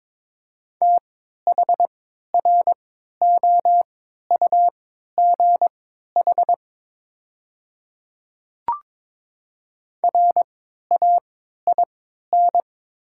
Morse code, heard as a keyed tone, about 22 wpm, spaced as for 12 wpm, 700 Hz. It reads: THROUGH RAIN
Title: 3x Character Spacing - Top 500 Words - 22wpm